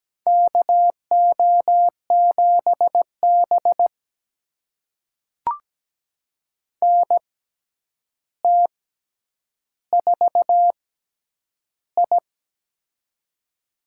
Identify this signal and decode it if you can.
Morse code, heard as a keyed tone, about 17 wpm, spaced as for 9 wpm, 700 Hz. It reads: KO7B NT4I
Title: USA Call Signs - 4 Characters - 6x Character Spacing - 17wpm